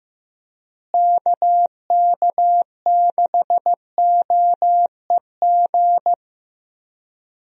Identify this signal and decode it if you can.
Morse code, heard as a keyed tone, about 15 wpm, 700 Hz. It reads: KK6OEG
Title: USA Call Signs - 6 Characters - Repeated 3x - 15wpm